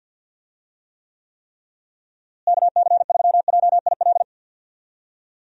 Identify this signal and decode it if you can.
Morse code, heard as a keyed tone, about 51 wpm, 700 Hz. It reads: KC3JIL
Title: USA Call Signs - 6 Characters - Repeated 3x - 50wpm